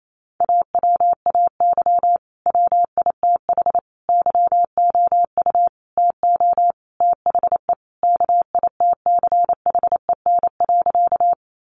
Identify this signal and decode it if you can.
Morse code, heard as a keyed tone, about 28 wpm, 700 Hz. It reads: AWAY WITH YOU TO THE KITCHEN.
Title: Sentences from Top 1400 Words 28wpm